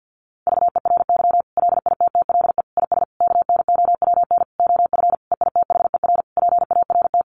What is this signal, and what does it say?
Morse code, heard as a keyed tone, about 50 wpm, 700 Hz. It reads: VERY LITTLE IS KNOWN OF EITHER PART
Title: Sentences from Top 1500 Words 50wpm